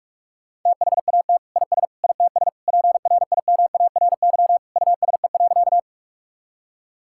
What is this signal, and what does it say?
Morse code, heard as a keyed tone, about 46 wpm, 700 Hz. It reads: THAT IS ITS PRIMARY USE.
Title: Sentences from Top 1000 Words - Review Part 5 - 45wpm